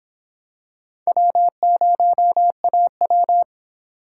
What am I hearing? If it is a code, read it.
Morse code, heard as a keyed tone, about 26 wpm, 700 Hz. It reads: W0AW